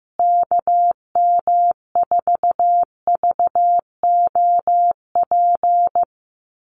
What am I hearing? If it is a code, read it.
Morse code, heard as a keyed tone, about 15 wpm, 700 Hz. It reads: KM4VOP
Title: USA Call Signs Repeated 2x 15wpm